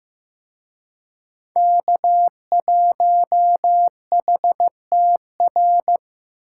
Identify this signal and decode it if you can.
Morse code, heard as a keyed tone, about 18 wpm, 700 Hz. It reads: K1HTR
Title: Speed Racing - USA Call Signs 15wpm